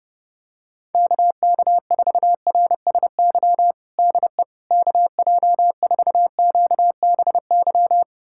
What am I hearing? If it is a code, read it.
Morse code, heard as a keyed tone, about 31 wpm, 700 Hz. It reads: KK4RSY DE KJ4QBY
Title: USA Callsign DE Callsign - 6 Characters - 30wpm